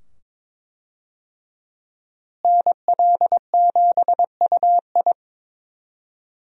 Morse code, heard as a keyed tone, about 22 wpm, 700 Hz. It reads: NL7UI
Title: USA Call Signs - 5 Characters - Repeated 2x - 22wpm